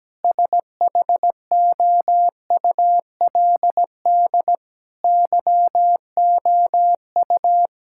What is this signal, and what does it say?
Morse code, heard as a keyed tone, about 17 wpm, 700 Hz. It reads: SHOULD YOU
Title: Sentences from Top 1000 Words - Review Part 6 - 17wpm